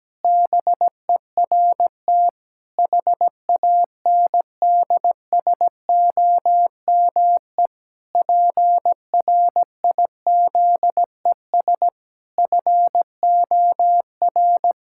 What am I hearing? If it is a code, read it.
Morse code, heard as a keyed tone, about 17 wpm, 700 Hz. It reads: BERT HANDSOME PRIZES FOR
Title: Pangram Challenge v2 - 17wpm